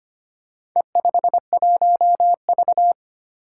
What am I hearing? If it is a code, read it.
Morse code, heard as a keyed tone, about 25 wpm, 700 Hz. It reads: E51V
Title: Four Challenging Letter-Number Combinations 25wpm